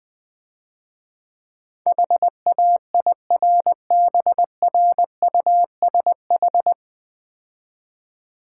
Morse code, heard as a keyed tone, about 20 wpm, 700 Hz. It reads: HAIRBRUSH
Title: Easy Words - Speed-Racing with Hints - 20wpm